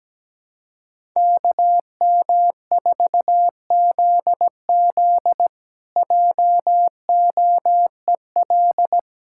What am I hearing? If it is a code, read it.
Morse code, heard as a keyed tone, about 17 wpm, 700 Hz. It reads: KM4ZZ JOEL
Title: K1USN SST QSO Practice - 17wpm